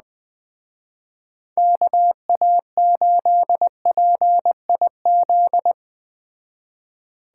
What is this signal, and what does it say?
Morse code, heard as a keyed tone, about 20 wpm, 700 Hz. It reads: KA8PIZ